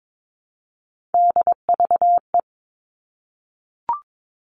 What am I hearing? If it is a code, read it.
Morse code, heard as a keyed tone, about 22 wpm, 700 Hz. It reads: DVE
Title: All Cut Numbers - 3-Digit 22wpm